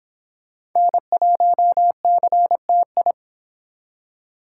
Morse code, heard as a keyed tone, about 26 wpm, 700 Hz. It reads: N1CTI